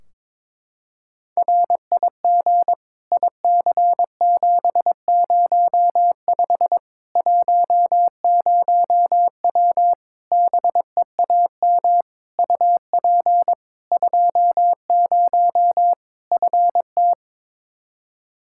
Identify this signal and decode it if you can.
Morse code, heard as a keyed tone, about 22 wpm, 700 Hz. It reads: RIG IC705 10W BEAM UP 20 FT
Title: Practice QSO - Exchange 2 - Info - 22wpm